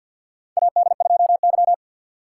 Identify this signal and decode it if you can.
Morse code, heard as a keyed tone, about 52 wpm, 690 Hz. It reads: AD2Y
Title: USA Call Signs - 6x Character Spacing - 50wpm